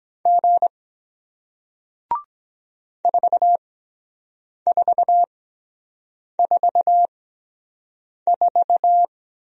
Morse code, heard as a keyed tone, about 23 wpm, spaced as for 7 wpm, 700 Hz. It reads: G 4444